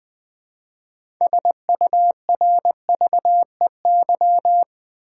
Morse code, heard as a keyed tone, about 20 wpm, 700 Hz. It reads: SURVEY
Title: 6 Letter Words - First 3 Letters Given - 20wpm